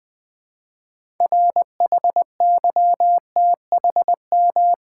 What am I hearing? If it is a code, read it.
Morse code, heard as a keyed tone, about 20 wpm, 700 Hz. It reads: RHYTHM
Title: 6 Letter Words - First 3 Letters Given - 20wpm